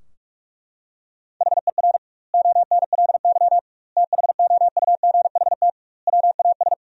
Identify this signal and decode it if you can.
Morse code, heard as a keyed tone, about 41 wpm, 700 Hz. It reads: HER ONLY THOUGHT WAS
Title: 2x Word-Spacing - Sentences from Top 400 Words - 45wpm